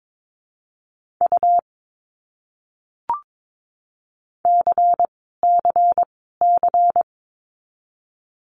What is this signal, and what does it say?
Morse code, heard as a keyed tone, about 22 wpm, 700 Hz. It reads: U C C C
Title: Abbreviations Repeated 3x - 22wpm